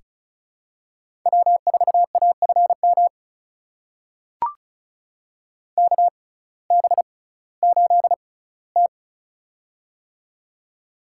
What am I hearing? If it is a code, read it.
Morse code, heard as a keyed tone, about 36 wpm, spaced as for 19 wpm, 700 Hz. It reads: W4AFM KB8T
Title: USA Call Signs - 6x Character Spacing - 35wpm